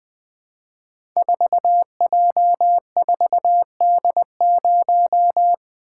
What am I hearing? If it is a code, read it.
Morse code, heard as a keyed tone, about 20 wpm, 700 Hz. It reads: 4J4D0
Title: Code Groups - 5 Letter:Numbers - 20wpm